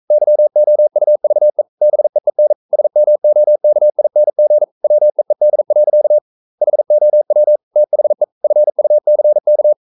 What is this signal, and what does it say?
Morse code, heard as a keyed tone, about 42 wpm, 600 Hz. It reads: YOUVE BEEN SMOKING WEED. HOW THE FUCK
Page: Romanen Never af Ken Follet på engelsk.